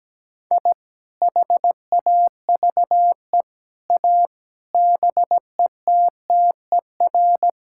Morse code, heard as a keyed tone, about 17 wpm, 700 Hz. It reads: I HAVE A BETTER
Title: Sentences from Top 1000 Words - Review Part 5 - 17wpm